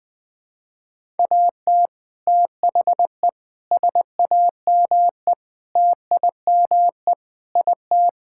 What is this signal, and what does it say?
Morse code, heard as a keyed tone, about 20 wpm, 700 Hz. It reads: AT THE SAME TIME IT
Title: Sentences with 3-Word Phrases - 20wpm